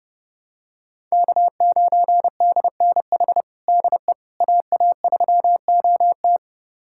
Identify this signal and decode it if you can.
Morse code, heard as a keyed tone, about 30 wpm, 700 Hz. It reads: K9DNH DE AA3OT
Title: USA Callsign DE Callsign - 5 Characters - 30wpm